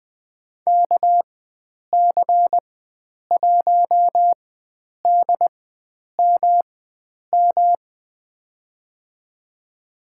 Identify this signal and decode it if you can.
Morse code, heard as a keyed tone, about 20 wpm, spaced as for 9 wpm, 700 Hz. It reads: KC1DMM
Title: USA Call Signs - 6 Characters - 4x Character Spacing - 20wpm